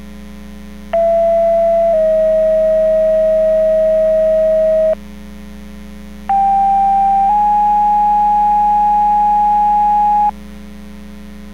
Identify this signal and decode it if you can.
Two-tone page, heard as a one-second tone, then a three-second tone, about 670 then 640 Hz, about 790 then 830 Hz. It is heard from 0.9 to 10.3 s.